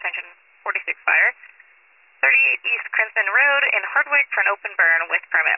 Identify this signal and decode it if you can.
Emergency Alert System (SAME) tones, heard as harsh buzzing bursts, with speech. It is heard from 1.2 to 5.1 s.